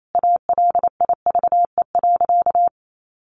Morse code, heard as a keyed tone, about 28 wpm, 700 Hz. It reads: ALIVE.
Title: A Sound Of Thunder 28wpm